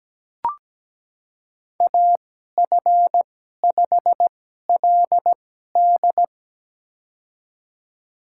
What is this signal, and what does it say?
Morse code, heard as a keyed tone, about 17 wpm, spaced as for 12 wpm, 700 Hz. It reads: AF5LD
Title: USA Call Signs - 5 Characters - 2x Character Spacing - 17wpm